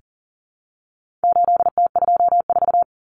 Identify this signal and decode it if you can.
Morse code, heard as a keyed tone, about 40 wpm, 700 Hz. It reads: 8T24
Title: Common Cut Numbers - 4-Digit 40wpm